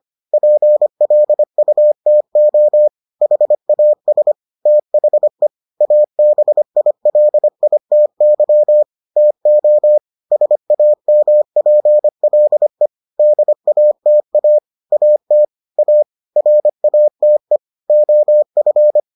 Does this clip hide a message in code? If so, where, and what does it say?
Yes: Morse code from 0.3 to 19.0 s, heard as a keyed tone, about 25 wpm, 600 Hz. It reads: PLUTO HAS THE ABILITY TO SAMPLE DATA AT A RATE OF